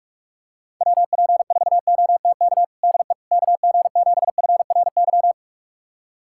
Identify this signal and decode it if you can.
Morse code, heard as a keyed tone, about 46 wpm, 700 Hz. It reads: WP4OTK DE KG7FRY
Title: USA Callsign DE Callsign - 6 Characters - 45wpm